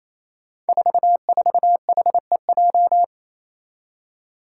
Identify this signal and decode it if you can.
Morse code, heard as a keyed tone, about 28 wpm, 700 Hz. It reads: 44HEJ
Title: Five Challenging Letter-Number Combinations 28wpm